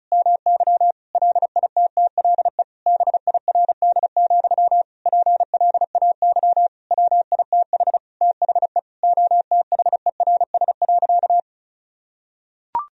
Morse code, heard as a keyed tone, about 36 wpm, 700 Hz. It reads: MY LITTLE BIRD, PLAY WITH THE OTHERS.
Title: Sentences from Top 1000 Words - Review Part 2 - 35wpm